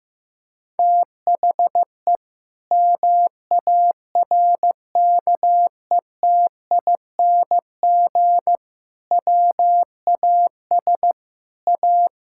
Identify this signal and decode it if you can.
Morse code, heard as a keyed tone, about 15 wpm, 700 Hz. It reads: THE MARKETING WAS A
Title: Sentences from Top 1000 Words - Review Part 5 - 15wpm